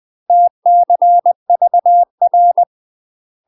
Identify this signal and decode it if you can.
Morse code, heard as a keyed tone, about 20 wpm, 700 Hz. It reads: TCVR
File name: UK MASTER QSOs TRAINING/20WPM